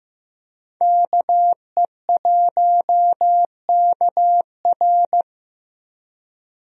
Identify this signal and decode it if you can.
Morse code, heard as a keyed tone, about 16 wpm, 700 Hz. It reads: KE1KR
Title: USA Call Signs - 5 Characters - Repeated 2x - 15wpm